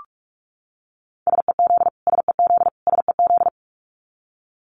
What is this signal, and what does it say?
Morse code, heard as a keyed tone, about 44 wpm, 700 Hz. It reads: SEZ SEZ SEZ
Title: Abbreviations Repeated 3x - 45wpm